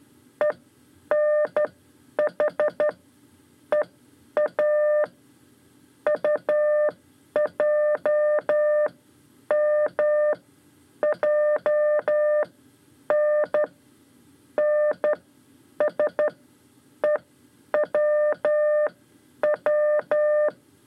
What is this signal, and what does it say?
Morse code, heard as a keyed tone, about 12 wpm, spaced as for 8 wpm, 600 Hz. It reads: ENHEA UJMJN NSEWW
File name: 5 Lezione CW 2